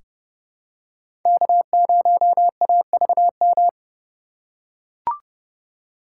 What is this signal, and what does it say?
Morse code, heard as a keyed tone, about 32 wpm, 700 Hz. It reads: K0AVM
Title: USA Call Signs - 5 Characters - 2x Character Spacing - 30wpm